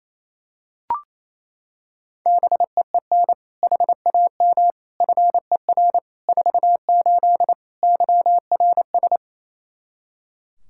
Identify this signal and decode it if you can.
Morse code, heard as a keyed tone, about 28 wpm, 700 Hz. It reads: BEEN HAM FER 48 YRS
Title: Practice QSO - Exchange 3 - More Info - 28wpm